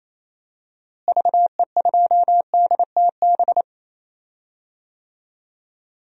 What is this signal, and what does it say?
Morse code, heard as a keyed tone, about 28 wpm, 700 Hz. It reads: VE2DTB
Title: DXCC-100 Call Signs - 6 Characters - Repeated 3x - 28wpm